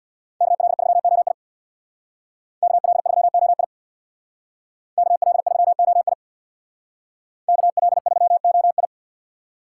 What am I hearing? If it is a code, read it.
Morse code, heard as a keyed tone, about 63 wpm, 700 Hz. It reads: KL3OI KL3OI KL3OI KL3OI